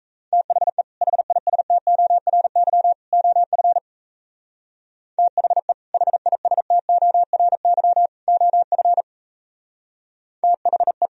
Morse code, heard as a keyed tone, about 32 wpm, 700 Hz. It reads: THE HISTORY OF THE HISTORY OF THE
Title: Speed-Racing - Sets of 3 Words 28wpm